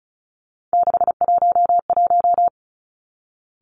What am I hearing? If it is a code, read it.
Morse code, heard as a keyed tone, about 36 wpm, 700 Hz. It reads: 611
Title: Three Challenging Letter-Number Combinations 35wpm